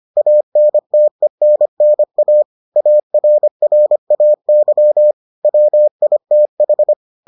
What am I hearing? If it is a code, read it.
Morse code, heard as a keyed tone, about 25 wpm, 600 Hz. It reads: ANTENNA ARRAY WITH